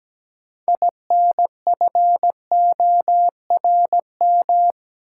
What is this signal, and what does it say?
Morse code, heard as a keyed tone, about 17 wpm, 700 Hz. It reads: INFORM
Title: Six Letter Words - 17wpm